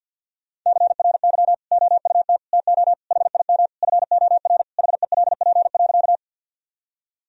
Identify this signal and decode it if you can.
Morse code, heard as a keyed tone, about 47 wpm, 700 Hz. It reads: CRY OUT TO HIM FOR HELP.
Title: Sentences from Top 1000 Words - Review Part 2 - 50wpm